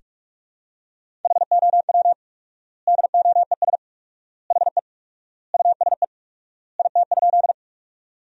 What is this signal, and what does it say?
Morse code, heard as a keyed tone, about 45 wpm, 700 Hz. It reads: HOW DOES HE USE IT?
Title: 4x Word-Spacing - Sentences from Top 200 Words - 45wpm